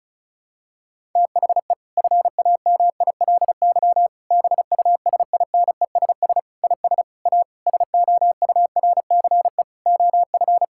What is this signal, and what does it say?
Morse code, heard as a keyed tone, about 36 wpm, 700 Hz. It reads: THE FAMILY BUSINESS IS A SOURCE OF
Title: Sentences from Top 1000 Words - Review Part 6 - 35wpm